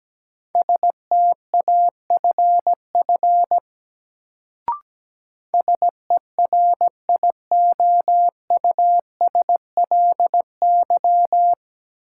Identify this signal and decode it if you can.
Morse code, heard as a keyed tone, about 17 wpm, 700 Hz. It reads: STAFF SERIOUSLY
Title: Sentences from Top 1200 Words - New Word Drill - 17wpm